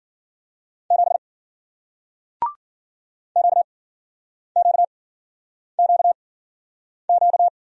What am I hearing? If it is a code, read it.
Morse code, heard as a keyed tone, about 54 wpm, spaced as for 9 wpm, 700 Hz. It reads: 7 QQQQ